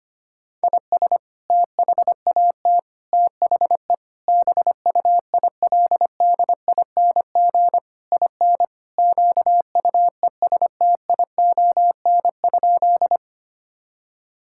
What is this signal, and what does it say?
Morse code, heard as a keyed tone, about 25 wpm, 700 Hz. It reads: IS THAT THE BUILDING IN QUESTION?
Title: Sentences from Top 1000 Words - Review Part 5 - 25wpm